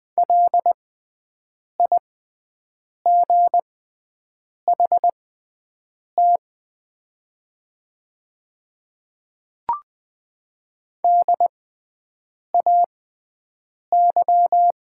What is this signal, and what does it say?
Morse code, heard as a keyed tone, about 20 wpm, spaced as for 7 wpm, 700 Hz. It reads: LIGHT DAY
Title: 6x Character Spacing - Top 200 Words - 20wpm